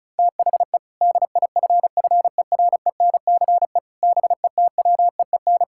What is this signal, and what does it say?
Morse code, heard as a keyed tone, about 36 wpm, 700 Hz. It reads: THE DIFFERENCE BETWEEN
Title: Sentences from Top 1600 Words 35wpm